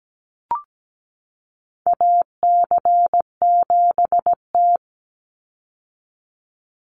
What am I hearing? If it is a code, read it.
Morse code, heard as a keyed tone, about 17 wpm, 700 Hz. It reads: AC7T
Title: Call Signs - 17wpm